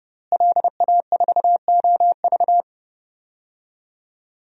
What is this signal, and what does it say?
Morse code, heard as a keyed tone, about 30 wpm, 700 Hz. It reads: LA4OV
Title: DXCC-100 Call Signs - 5 Characters - Repeated 3x - 30wpm